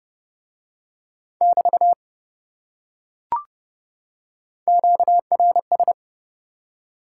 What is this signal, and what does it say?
Morse code, heard as a keyed tone, about 27 wpm, 700 Hz. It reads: <BT> QRS